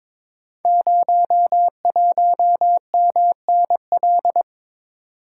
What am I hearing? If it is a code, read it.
Morse code, heard as a keyed tone, about 22 wpm, 700 Hz. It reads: 01MNL
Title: Code Groups - 5 Letter:Numbers - 22wpm